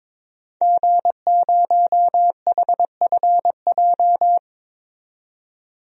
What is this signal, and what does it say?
Morse code, heard as a keyed tone, about 22 wpm, 700 Hz. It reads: G0HFJ
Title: DXCC-100 Call Signs - 5 Characters - 22wpm